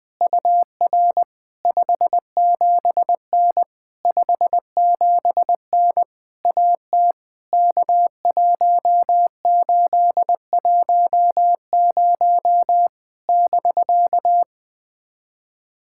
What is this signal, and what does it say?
Morse code, heard as a keyed tone, about 20 wpm, 700 Hz. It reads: UR 57N 57N AT K1810 <BK>
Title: POTA QSO Practice - 20wpm